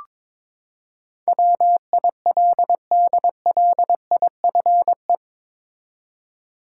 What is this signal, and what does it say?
Morse code, heard as a keyed tone, about 22 wpm, 700 Hz. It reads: WILDLIFE
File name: practice / 22wpm / Eight Letter Words - 22wpm